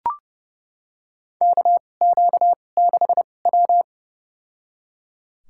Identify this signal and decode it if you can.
Morse code, heard as a keyed tone, about 31 wpm, spaced as for 22 wpm, 700 Hz. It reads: KQ6W
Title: USA Call Signs - 4 Characters - 2x Character Spacing - 30wpm